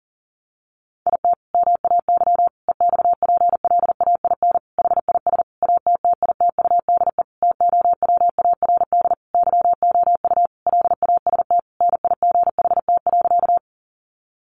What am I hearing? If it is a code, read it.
Morse code, heard as a keyed tone, about 40 wpm, 700 Hz. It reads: IT MAY EXPLAIN HIS ATTITUDE TOWARD YOU LAST NIGHT.
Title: Sentences from Top 1300 Words 40wpm